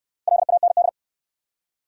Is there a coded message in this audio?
Yes: Morse code from 0.3 to 0.9 s, heard as a keyed tone, about 68 wpm, 700 Hz. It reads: LAND